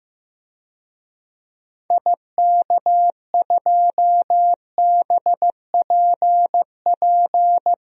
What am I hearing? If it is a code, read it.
Morse code, heard as a keyed tone, about 15 wpm, 700 Hz. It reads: IK2BPP
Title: DXCC-100 Call Signs - 6 Characters - 15wpm